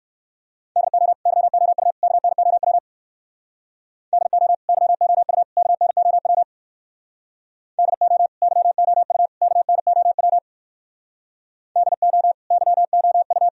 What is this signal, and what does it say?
Morse code, heard as a keyed tone, about 59 wpm, 700 Hz. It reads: DO YOU KNOW DO YOU KNOW DO YOU KNOW DO YOU